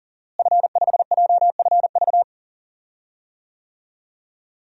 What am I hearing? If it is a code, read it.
Morse code, heard as a keyed tone, about 40 wpm, 700 Hz. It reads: F5JFV
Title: DXCC-100 Call Signs - 5 Characters - Repeated 2x - 40wpm